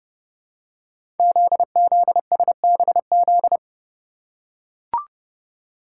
Morse code, heard as a keyed tone, about 30 wpm, 700 Hz. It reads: ZZSBZ
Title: Code Groups - 5 Letters - 30wpm